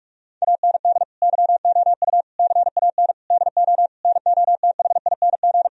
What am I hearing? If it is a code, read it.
Morse code, heard as a keyed tone, about 46 wpm, 700 Hz. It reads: AND YOU CAN DO NOTHING
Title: Sentences from Top 1000 Words - Review Part 1 - 45wpm